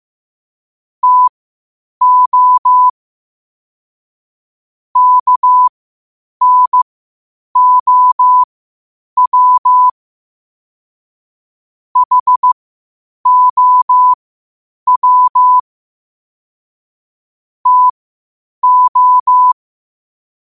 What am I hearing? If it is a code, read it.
Morse code, heard as a keyed tone, about 15 wpm, spaced as for 8 wpm, 1000 Hz. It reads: TO KNOW HOW TO
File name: qotdfiles / 8wpm